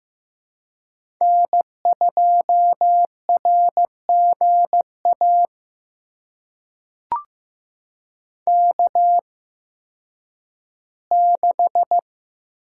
Morse code, heard as a keyed tone, about 15 wpm, spaced as for 10 wpm, 700 Hz. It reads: N2RGA K6
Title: USA Call Signs - 8x Character Spacing - 15wpm